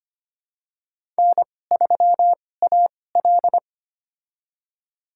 Morse code, heard as a keyed tone, about 25 wpm, spaced as for 18 wpm, 700 Hz. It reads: N3AL
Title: USA Call Signs - 2x Character Spacing - 25wpm